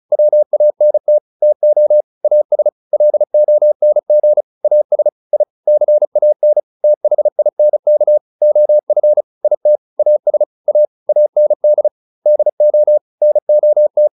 Morse code, heard as a keyed tone, about 34 wpm, 600 Hz. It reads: WANT TO AS LONG AS I CAN THINK OF IT AS A AND DO NOT